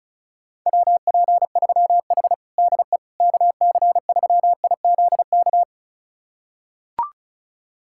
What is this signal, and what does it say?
Morse code, heard as a keyed tone, about 34 wpm, 700 Hz. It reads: WP3H DE KC3IZK
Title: USA Callsign DE Callsign 35wpm